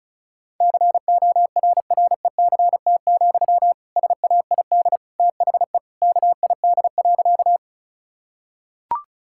Morse code, heard as a keyed tone, about 36 wpm, 700 Hz. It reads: CORRECT, SAID THE KID.
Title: Sentences from Top 1600 Words 35wpm